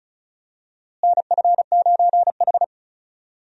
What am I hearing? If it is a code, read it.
Morse code, heard as a keyed tone, about 34 wpm, 700 Hz. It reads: NF9H
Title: USA Call Signs - 4 Characters - Repeated 2x - 35wpm